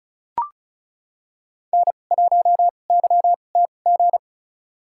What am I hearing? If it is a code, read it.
Morse code, heard as a keyed tone, about 36 wpm, spaced as for 25 wpm, 700 Hz. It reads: N1YTG
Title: USA Call Signs - 5 Characters - 2x Character Spacing - 35wpm